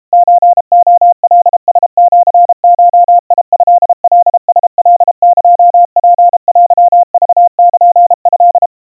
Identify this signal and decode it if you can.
Morse code (Japanese Wabun code), heard as a keyed tone, about 32 wpm, 700 Hz. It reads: ソレカラシゴトカラカエツテクルト
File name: kawa / Morse sample / kamyu 120 cpm 700 Hz Crispy